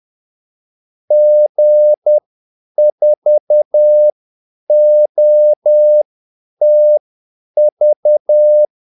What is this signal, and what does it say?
Morse code, heard as a keyed tone, about 10 wpm, spaced as for 8 wpm, 600 Hz. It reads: G4OTV
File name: Callsigns-10-8wpm